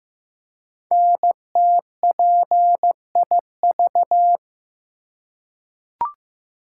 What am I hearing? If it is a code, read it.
Morse code, heard as a keyed tone, about 15 wpm, 700 Hz. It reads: NTPIV
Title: Code Groups - 5 Letter:Numbers - 15wpm